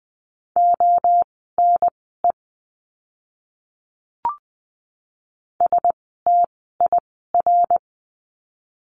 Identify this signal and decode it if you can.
Morse code, heard as a keyed tone, about 20 wpm, spaced as for 15 wpm, 700 Hz. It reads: ONE STIR
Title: 068 Lesson 12f - Letter d - Words 20wpm